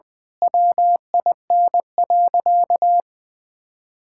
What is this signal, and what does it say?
Morse code, heard as a keyed tone, about 20 wpm, 700 Hz. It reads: WIN.
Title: Sentences with 8-Word Phrases - Spoken Beginning - 20wpm